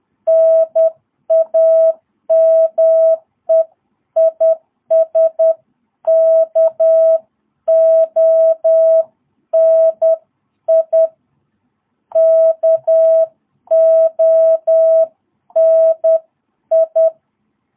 Morse code, heard as a keyed tone, about 10 wpm, 660 Hz. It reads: NAME IS KONI KONI
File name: jh8cbh / 2022cw